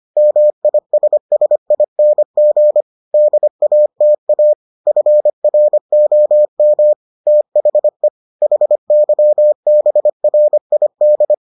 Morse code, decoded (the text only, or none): MISSING DATA FROM THE HYBRID